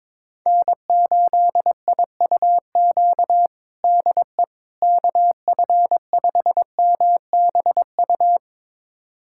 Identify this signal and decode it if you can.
Morse code, heard as a keyed tone, about 22 wpm, 700 Hz. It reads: N8IUQ DE KF5MBU
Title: USA Callsign DE Callsign 22wpm